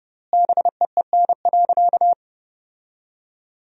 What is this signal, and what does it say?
Morse code, heard as a keyed tone, about 30 wpm, 700 Hz. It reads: BEEN.
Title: 3x Word-Spacing - Sentences from Top 300 Words - 30wpm